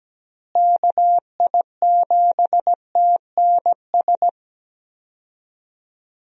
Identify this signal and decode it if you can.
Morse code, heard as a keyed tone, about 17 wpm, 700 Hz. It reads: KI7TNS
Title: USA Call Signs - 6 Characters - Repeated 3x - 17wpm